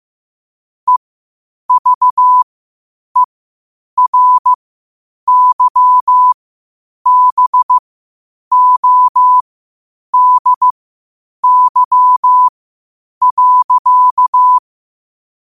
Morse code, heard as a keyed tone, about 15 wpm, spaced as for 8 wpm, 1000 Hz. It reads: EVERYBODY.
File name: qotdfiles / 8wpm